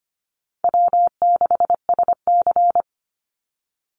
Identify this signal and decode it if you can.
Morse code, heard as a keyed tone, about 25 wpm, 700 Hz. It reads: W6SC
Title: Call Signs Repeated 2x - 25wpm